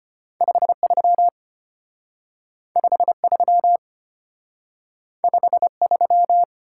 Morse code, heard as a keyed tone, about 35 wpm, 700 Hz. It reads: 53 53 53